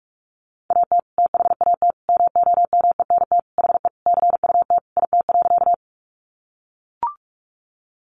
Morse code, heard as a keyed tone, about 44 wpm, 700 Hz. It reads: AT THAT MOMENT HE CUT IT.
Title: Sentences from Top 1000 Words 45wpm